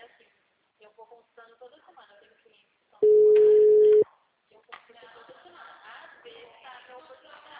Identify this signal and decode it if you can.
Ringback tone (Europe, 425 Hz), with speech.